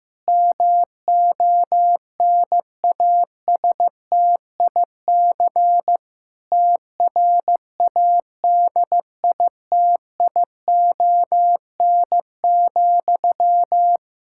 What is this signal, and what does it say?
Morse code, heard as a keyed tone, about 15 wpm, 700 Hz. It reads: MONASTIC TRADITION,
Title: Pangram Challenge v2 - 15wpm